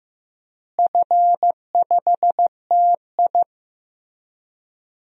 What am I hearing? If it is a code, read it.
Morse code, heard as a keyed tone, about 15 wpm, 700 Hz. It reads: F5TI
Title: DXCC-100 Call Signs - 4 Characters - Repeated 3x - 15wpm